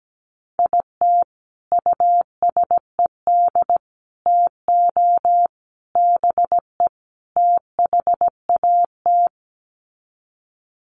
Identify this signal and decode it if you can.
Morse code, heard as a keyed tone, about 17 wpm, 700 Hz. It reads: IT USED TO BE THAT
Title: Sets of 5 Words Encore2 17wpm